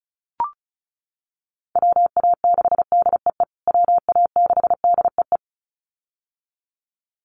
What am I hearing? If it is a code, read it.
Morse code, heard as a keyed tone, about 34 wpm, 700 Hz. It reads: WA6DEE WA6DEE
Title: USA Call Signs Repeated 2x 35wpm